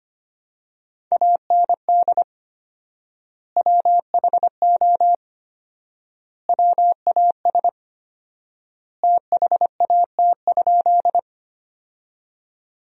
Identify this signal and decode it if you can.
Morse code, heard as a keyed tone, about 25 wpm, 700 Hz. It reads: AND WHO WAS THAT?
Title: 4x Word-Spacing - Sentences from Top 100 Words - 25wpm